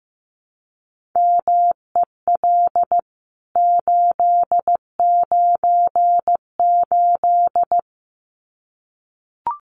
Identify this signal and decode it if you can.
Morse code, heard as a keyed tone, about 15 wpm, 700 Hz. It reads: MEL 898
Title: CWT Exchange 15wpm